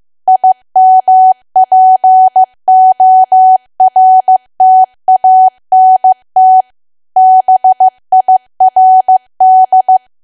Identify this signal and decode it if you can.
Morse code, heard as a keyed tone, about 15 wpm, 750 Hz. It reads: IMPORTANT BIRD
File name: cwa / W1AW-Level2 / 15WPM